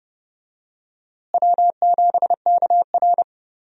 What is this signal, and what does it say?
Morse code, heard as a keyed tone, about 30 wpm, 700 Hz. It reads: W7KR
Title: USA Call Signs - 4 Characters - Repeated 2x - 30wpm